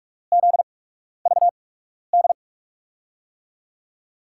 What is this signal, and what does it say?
Morse code, heard as a keyed tone, about 42 wpm, spaced as for 12 wpm, 700 Hz. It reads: ZVD
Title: USA Call Signs - 6 Characters - 8x Character Spacing - 45wpm